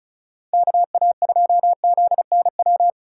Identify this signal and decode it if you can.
Morse code, heard as a keyed tone, about 36 wpm, 700 Hz. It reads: KA2ZNW
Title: USA Call Signs - 6 Characters - 4x Character Spacing - 35wpm